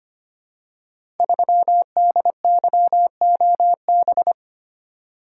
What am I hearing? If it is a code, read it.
Morse code, heard as a keyed tone, about 25 wpm, 700 Hz. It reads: 3DYOB
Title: Code Groups - 5 Letter:Numbers - 25wpm